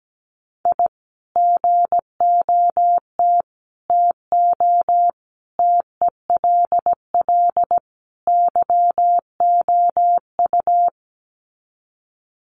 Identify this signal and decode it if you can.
Morse code, heard as a keyed tone, about 17 wpm, 700 Hz. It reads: I GOT TO TELL YOU
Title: Sets of 5 Words Encore2 17wpm